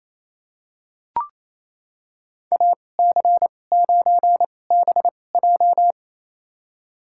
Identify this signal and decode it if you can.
Morse code, heard as a keyed tone, about 28 wpm, spaced as for 20 wpm, 700 Hz. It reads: AC9BJ
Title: USA Call Signs - 5 Characters - 2x Character Spacing - 28wpm